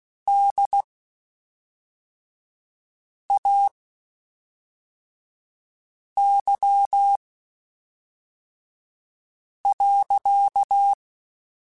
Morse code, heard as a keyed tone, about 16 wpm, spaced as for 3 wpm, 800 Hz. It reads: DAY.
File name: Spaced Characters / Spaced Characters-mp3s / Weather5 12 sp @5